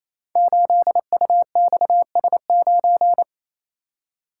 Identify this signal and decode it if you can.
Morse code, heard as a keyed tone, about 29 wpm, 690 Hz. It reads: 8UXS9
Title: Code Groups - 5 Letter:Numbers - 28wpm